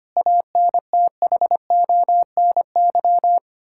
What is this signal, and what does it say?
Morse code, heard as a keyed tone, about 25 wpm, 700 Hz. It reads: ANTHONY
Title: Practice QSO - Exchange 1 - Beginning - 25wpm